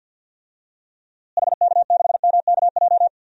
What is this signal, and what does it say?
Morse code, heard as a keyed tone, about 50 wpm, 700 Hz. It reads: HK6MKJ
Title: DXCC-100 Call Signs - 6 Characters - Repeated 3x - 50wpm